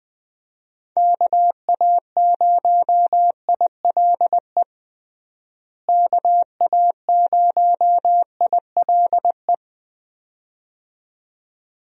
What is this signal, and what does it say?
Morse code, heard as a keyed tone, about 20 wpm, 700 Hz. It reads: KA0ILE KA0ILE
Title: USA Call Signs - 6 Characters - Repeated 2x - 20wpm